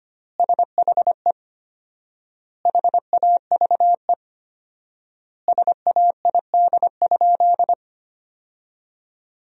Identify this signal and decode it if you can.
Morse code, heard as a keyed tone, about 25 wpm, 700 Hz. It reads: SHE HAVE SAID?
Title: 4x Word-Spacing - Sentences from Top 100 Words - 25wpm